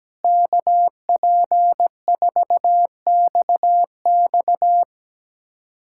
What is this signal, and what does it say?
Morse code, heard as a keyed tone, about 17 wpm, 700 Hz. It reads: KP4XX
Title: USA Callsign DE Callsign - 5 Characters - 17wpm